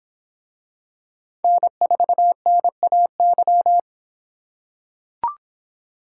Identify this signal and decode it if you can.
Morse code, heard as a keyed tone, about 25 wpm, 700 Hz. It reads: N4NAY